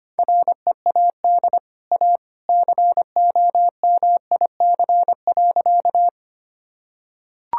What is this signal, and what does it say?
Morse code, heard as a keyed tone, about 25 wpm, 700 Hz. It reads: READ A COMIC.
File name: Easy Sentences with Hints - 25wpm